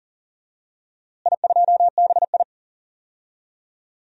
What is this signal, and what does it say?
Morse code, heard as a keyed tone, about 40 wpm, 700 Hz. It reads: I2BI
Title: DXCC-100 Call Signs - 4 Characters - Repeated 2x - 40wpm